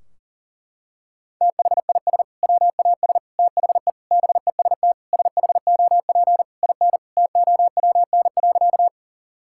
Morse code, heard as a keyed tone, about 40 wpm, 700 Hz. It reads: THIS WAS THE BEST SHOP IN TOWN.
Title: Sentences from Top 1600 Words 40wpm